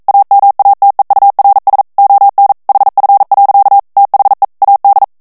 Morse code, heard as a keyed tone, about 42 wpm, 800 Hz. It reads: AMATEURS ON HF. THE AD